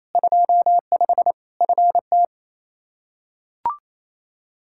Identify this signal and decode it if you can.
Morse code, heard as a keyed tone, about 28 wpm, 700 Hz. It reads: 25 FT
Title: Practice QSO - Exchange 2 - Info - 28wpm